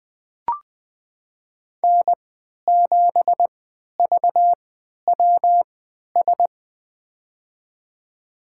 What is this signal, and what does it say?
Morse code, heard as a keyed tone, about 20 wpm, spaced as for 11 wpm, 700 Hz. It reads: N7VWS
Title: USA Call Signs - 5 Characters - 3x Character Spacing - 20wpm